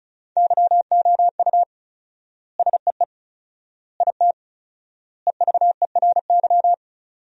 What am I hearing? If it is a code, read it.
Morse code, heard as a keyed tone, about 36 wpm, 700 Hz. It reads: YOU SEE IT EVERY
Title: 4x Word-Spacing - Sentences from Top 200 Words - 35wpm